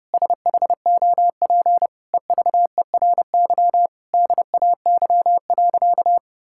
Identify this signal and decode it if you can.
Morse code, heard as a keyed tone, about 30 wpm, 700 Hz. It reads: SHOP EVERY DAY.
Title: Sentences from Top 1600 Words 30wpm